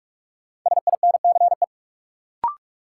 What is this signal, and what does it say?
Morse code, heard as a keyed tone, about 42 wpm, 700 Hz. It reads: SINCE
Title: Sentences from Top 300 Words - New Word Drill - 45wpm